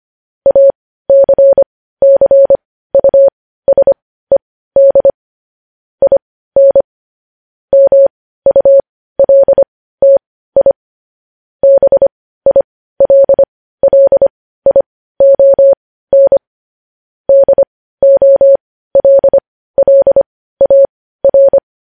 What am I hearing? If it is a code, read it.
Morse code, heard as a keyed tone, about 25 wpm, spaced as for 15 wpm, 550 Hz. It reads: ACCUSED IN MULTI BILLION DOLLAR